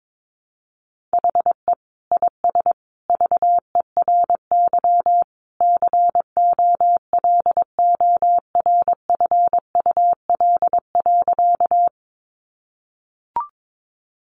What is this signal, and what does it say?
Morse code, heard as a keyed tone, about 22 wpm, 700 Hz. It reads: HE IS VERY COLORFUL.
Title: Easy Phrases 22wpm